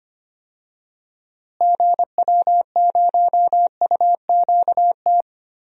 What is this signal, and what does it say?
Morse code, heard as a keyed tone, about 25 wpm, 700 Hz. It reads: GW0UQT
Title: DXCC-100 Call Signs - 6 Characters - Repeated 2x - 25wpm